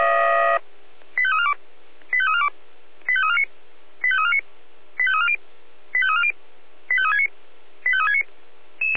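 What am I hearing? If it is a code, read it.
Five-tone selective call (ZVEI), heard as quick runs of five tones, tones of about 70 ms. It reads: 85342 85342 85349 85349 85340 85340 85369 85369
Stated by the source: Five-tone ZVEI selective call received by SDR.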